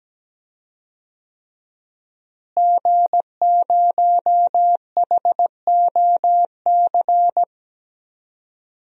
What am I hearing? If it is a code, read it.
Morse code, heard as a keyed tone, about 17 wpm, 700 Hz. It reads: G0HOC
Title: DXCC-100 Call Signs - 5 Characters - Repeated 3x - 17wpm